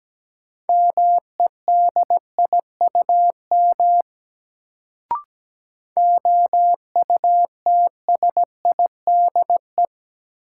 Morse code, heard as a keyed tone, about 17 wpm, 700 Hz. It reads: MEDIUM OUTSIDE
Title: Sentences from Top 900 Words - New Word Drill - 17wpm